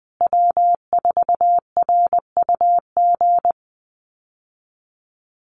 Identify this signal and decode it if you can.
Morse code, heard as a keyed tone, about 20 wpm, 700 Hz. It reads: W4RUG
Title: USA Call Signs Repeated 2x 20wpm